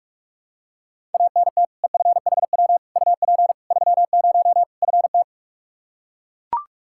Morse code, heard as a keyed tone, about 42 wpm, 700 Hz. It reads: ANT EFHW UP 30 FT
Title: Practice QSO - Exchange 2 - Info - 45wpm